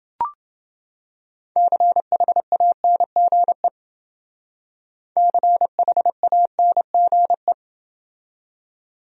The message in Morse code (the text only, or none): CHANGE CHANGE